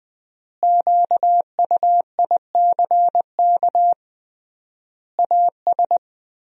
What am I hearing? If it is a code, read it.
Morse code, heard as a keyed tone, about 20 wpm, 700 Hz. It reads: QUICK AS
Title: 3x Word-Spacing - Sentences from Top 500 Words - 20wpm